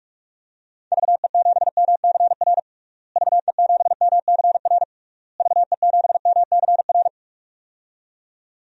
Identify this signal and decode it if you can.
Morse code, heard as a keyed tone, about 46 wpm, 700 Hz. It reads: VE7MCR VE7MCR VE7MCR
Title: DXCC-100 Call Signs - 6 Characters - Repeated 3x - 45wpm